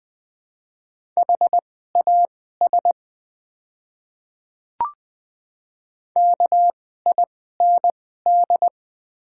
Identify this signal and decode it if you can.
Morse code, heard as a keyed tone, about 21 wpm, spaced as for 15 wpm, 700 Hz. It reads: HAS KIND
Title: 2x Character Spacing - Top 200 Words - 20wpm